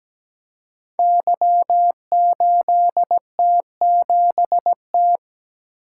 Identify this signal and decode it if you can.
Morse code, heard as a keyed tone, about 17 wpm, 700 Hz. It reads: Y8T7T
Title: Code Groups - 5 Letter:Numbers - 17wpm